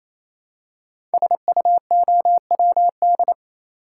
Morse code, heard as a keyed tone, about 27 wpm, 700 Hz. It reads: SUOWD
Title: Code Groups - 5 Letters - 28wpm